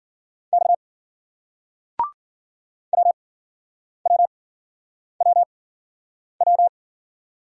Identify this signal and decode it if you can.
Morse code, heard as a keyed tone, about 56 wpm, spaced as for 9 wpm, 700 Hz. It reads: X WWWW